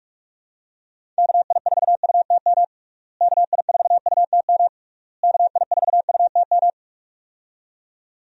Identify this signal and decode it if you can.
Morse code, heard as a keyed tone, about 42 wpm, 700 Hz. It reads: KI4UTM KI4UTM KI4UTM
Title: USA Call Signs - 6 Characters - Repeated 3x - 45wpm